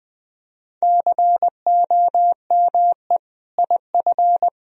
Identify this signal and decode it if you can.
Morse code, heard as a keyed tone, about 20 wpm, 700 Hz. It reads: COME IF
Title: Sentences from Top 1000 Words - Review Part 1 - 20wpm